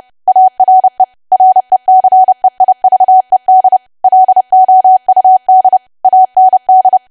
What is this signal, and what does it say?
Morse code, heard as a keyed tone, about 30 wpm, 750 Hz. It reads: ARE RECEIVED LOUD AND
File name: cwa / W1AW-Level3 / 30WPM